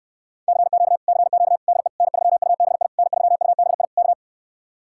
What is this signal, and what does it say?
Morse code, heard as a keyed tone, about 68 wpm, 700 Hz. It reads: CQ CQ DE N3UBI N3UBI K